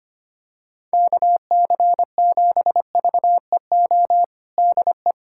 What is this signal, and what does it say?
Morse code, heard as a keyed tone, about 25 wpm, 700 Hz. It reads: KC7VEO DE
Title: USA Callsign DE Callsign - 6 Characters - 25wpm